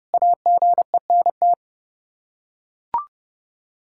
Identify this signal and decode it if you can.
Morse code, heard as a keyed tone, about 29 wpm, 700 Hz. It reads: AGENT
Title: Practice QSO - Exchange 3 - More Info - 30wpm